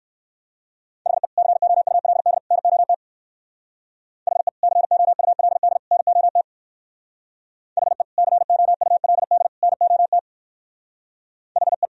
Morse code, heard as a keyed tone, about 61 wpm, 700 Hz. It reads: HE COULD NOT HE COULD NOT HE COULD NOT HE